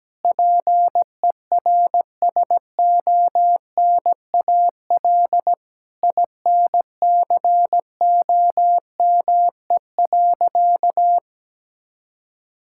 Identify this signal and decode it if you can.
Morse code, heard as a keyed tone, about 17 wpm, 700 Hz. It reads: PERSONAL INCOME.
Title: Sentences from Top 1000 Words - Review Part 5 - 17wpm